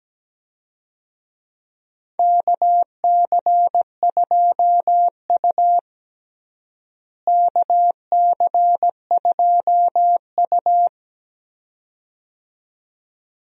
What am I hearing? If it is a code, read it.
Morse code, heard as a keyed tone, about 17 wpm, 700 Hz. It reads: KC2U KC2U
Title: USA Call Signs - 4 Characters - Repeated 2x - 17wpm